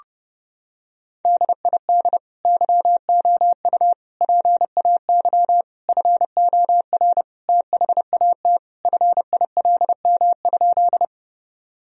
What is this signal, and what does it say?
Morse code, heard as a keyed tone, about 32 wpm, 700 Hz. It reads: DID YOU PAY FOR THAT FILM?
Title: Sentences from Top 1000 Words - Review Part 3 - 30wpm